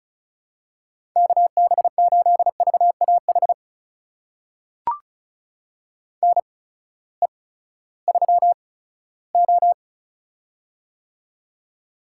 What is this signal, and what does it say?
Morse code, heard as a keyed tone, about 36 wpm, spaced as for 18 wpm, 700 Hz. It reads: KB8VAH NE3O
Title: USA Call Signs - 8x Character Spacing - 35wpm